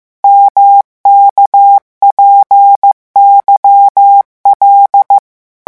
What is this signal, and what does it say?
Morse code, heard as a keyed tone, about 15 wpm, 800 Hz. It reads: MKPYL
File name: morse mp3 / revision11 lettres vitesse 14